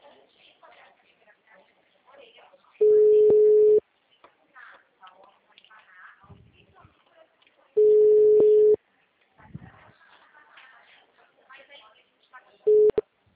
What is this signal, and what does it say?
Ringback tone (Europe, 425 Hz), with speech.
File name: out- -2013- - - .41306